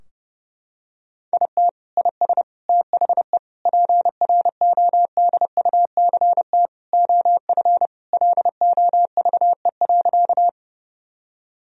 Morse code, heard as a keyed tone, about 32 wpm, 700 Hz. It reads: IT IS THE PRODUCT OF LOVE.
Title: Sentences from Top 1000 Words - Review Part 2 - 30wpm